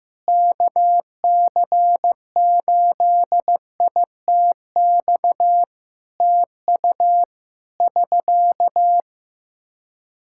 Morse code, heard as a keyed tone, about 15 wpm, 700 Hz. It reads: KC8ITX TU <SK>
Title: Practice QSO - Exchange 4 - Ending - 15wpm